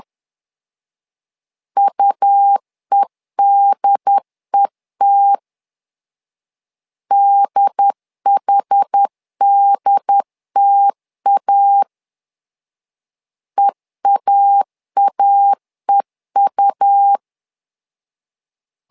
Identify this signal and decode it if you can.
Morse code, heard as a keyed tone, about 10 wpm, 800 Hz. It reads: UEDET DHDTA EAAEU
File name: LEZIONI / 2 UHDN / 2 UHD 10wpm 1826